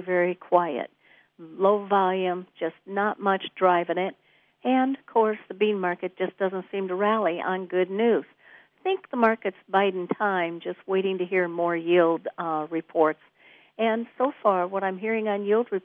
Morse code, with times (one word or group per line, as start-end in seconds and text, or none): none